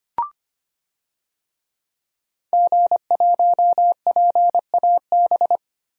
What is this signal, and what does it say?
Morse code, heard as a keyed tone, about 25 wpm, 700 Hz. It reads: G1PAB
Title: DXCC-100 Call Signs - 5 Characters - 25wpm